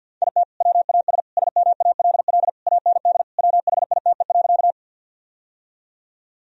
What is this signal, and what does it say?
Morse code, heard as a keyed tone, about 49 wpm, 700 Hz. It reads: IT WAS SMALL AND WHITE.
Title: Sentences from Top 1000 Words - Review Part 1 - 50wpm